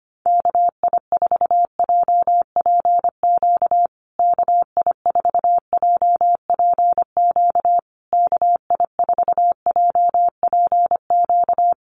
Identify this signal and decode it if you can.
Morse code, heard as a keyed tone, about 25 wpm, 700 Hz. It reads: KI4JPQ KI4JPQ KI4JPQ
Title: USA Call Signs Repeated 3x 25wpm